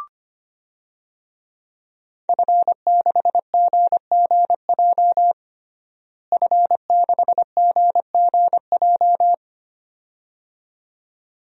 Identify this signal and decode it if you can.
Morse code, heard as a keyed tone, about 25 wpm, 700 Hz. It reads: F6GGJ F6GGJ
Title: DXCC-100 Call Signs - 5 Characters - Repeated 2x - 25wpm